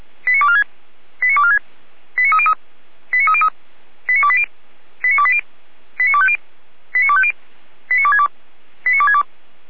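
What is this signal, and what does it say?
Five-tone selective call (ZVEI), heard as quick runs of five tones, tones of about 70 ms. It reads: 89246 89246 89303 89303 89280 89280 89250 89250 89262 89262
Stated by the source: ZVEI five-tone selective call received on SDR.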